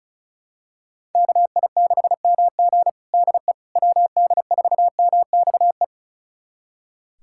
Morse code, heard as a keyed tone, about 34 wpm, 700 Hz. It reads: KI6MG DE WD4MXE
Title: USA Callsign DE Callsign 35wpm